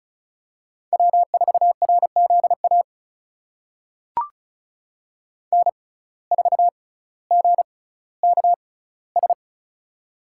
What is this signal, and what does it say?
Morse code, heard as a keyed tone, about 36 wpm, spaced as for 18 wpm, 700 Hz. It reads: W4RZA N4GKS
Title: USA Call Signs - 5 Characters - 6x Character Spacing - 35wpm